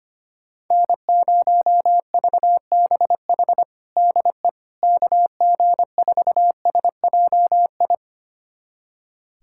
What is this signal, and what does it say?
Morse code, heard as a keyed tone, about 25 wpm, 700 Hz. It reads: N0VBH DE KG4SJI
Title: USA Callsign DE Callsign 25wpm